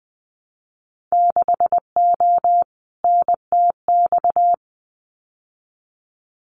Morse code, heard as a keyed tone, about 20 wpm, 700 Hz. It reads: 6O NTX
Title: Winter Field Day Exchanges Encore 20wpm